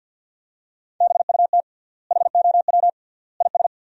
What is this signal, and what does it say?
Morse code, heard as a keyed tone, about 50 wpm, 700 Hz. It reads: BUT HOW IS
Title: 3x Word-Spacing - Sentences from Top 100 Words - 50wpm